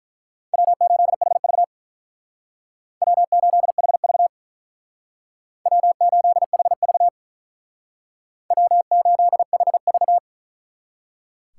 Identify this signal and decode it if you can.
Morse code, heard as a keyed tone, about 42 wpm, 700 Hz. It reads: W8HV W8HV W8HV W8HV
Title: Speed Racing - USA Call Signs 35wpm